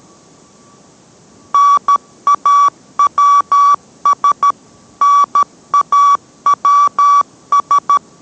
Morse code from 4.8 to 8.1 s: NAWS